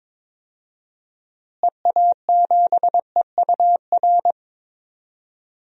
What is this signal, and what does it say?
Morse code, heard as a keyed tone, about 22 wpm, 700 Hz. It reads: EA7EUR
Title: DXCC-100 Call Signs - 6 Characters - Repeated 2x - 22wpm